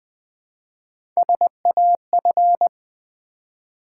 Morse code, heard as a keyed tone, about 20 wpm, 700 Hz. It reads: SAF
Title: Build A Word - 1 Letter At A Time - 4-Letter Words - 20wpm